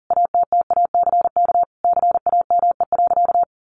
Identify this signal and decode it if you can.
Morse code, heard as a keyed tone, about 40 wpm, 700 Hz. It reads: ATTACK CAME.
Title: Sentences from Top 1500 Words 40wpm